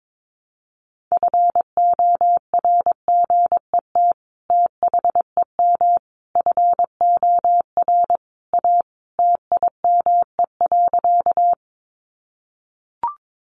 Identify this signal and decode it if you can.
Morse code, heard as a keyed tone, about 22 wpm, 700 Hz. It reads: FORGET THEM FOR A TIME.
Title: Sentences from Top 1200 Words 22wpm